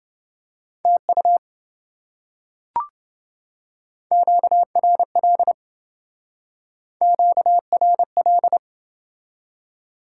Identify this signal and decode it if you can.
Morse code, heard as a keyed tone, about 27 wpm, 700 Hz. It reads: TU QRL QRL